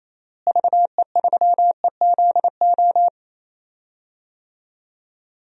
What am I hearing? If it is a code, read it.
Morse code, heard as a keyed tone, about 28 wpm, 700 Hz. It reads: VE3EZO
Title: DXCC-100 Call Signs - 6 Characters - Repeated 3x - 28wpm